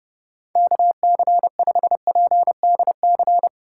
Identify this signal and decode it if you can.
Morse code, heard as a keyed tone, about 32 wpm, 700 Hz. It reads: KC5PDC
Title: USA Call Signs - 4x Character Spacing - 30wpm